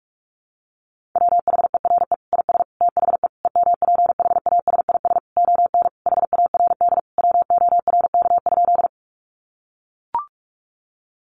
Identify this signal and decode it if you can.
Morse code, heard as a keyed tone, about 44 wpm, 700 Hz. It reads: WHERE IS THE EMPHASIS ON HARD WORK?
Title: Sentences from Top 1400 Words 45wpm